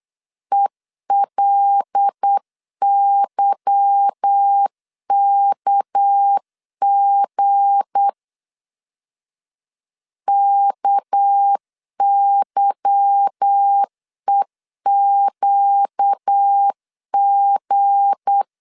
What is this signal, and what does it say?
Morse code, heard as a keyed tone, about 8 wpm, 800 Hz. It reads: ELYKG KYEQG
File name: LEZIONI / 8 YJ45 / 8 Y 8wpm 3245